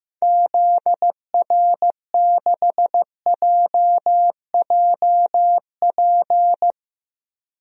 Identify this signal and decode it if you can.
Morse code, heard as a keyed tone, about 15 wpm, 700 Hz. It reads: ZR6JJP
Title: DXCC-100 Call Signs - 6 Characters - 15wpm